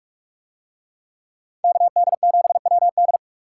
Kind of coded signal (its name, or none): Morse code